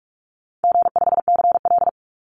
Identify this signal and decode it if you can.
Morse code, heard as a keyed tone, about 46 wpm, 700 Hz. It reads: G5CL
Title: Easy Call Signs - 45wpm